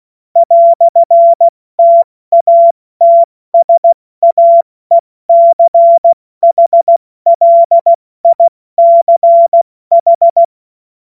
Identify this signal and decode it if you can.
Morse code, heard as a keyed tone, about 16 wpm, 680 Hz. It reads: "TATSAECHLICH
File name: Witz-22-Juli-2025-70-F80